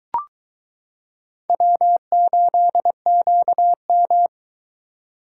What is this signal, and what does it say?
Morse code, heard as a keyed tone, about 22 wpm, 700 Hz. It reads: W8QM